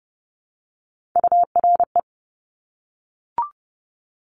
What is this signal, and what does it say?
Morse code, heard as a keyed tone, about 30 wpm, 700 Hz. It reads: URE
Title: Top Three Letter Combinations in Words 30wpm